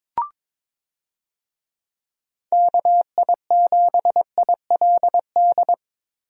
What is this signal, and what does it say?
Morse code, heard as a keyed tone, about 22 wpm, 700 Hz. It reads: KI7ILD
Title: USA Call Signs - 6 Characters - Repeated 3x - 22wpm